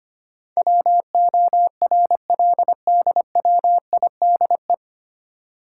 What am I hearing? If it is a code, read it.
Morse code, heard as a keyed tone, about 25 wpm, 700 Hz. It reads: WORLDWIDE
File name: practice / 25wpm / 9 Letter Words - First 3 Letters Given - 25wpm